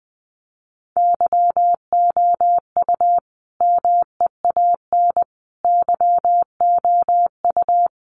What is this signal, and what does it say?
Morse code, heard as a keyed tone, about 20 wpm, 700 Hz. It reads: YOU MEAN YOU
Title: Sentences from Top 1400 Words 20wpm